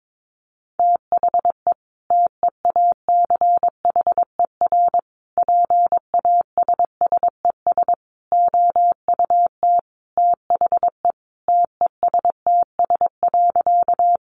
Morse code, heard as a keyed tone, about 22 wpm, 700 Hz. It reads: THE TEACHER PASSES OUT THE TESTS.
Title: Additional Easy Phrases 22wpm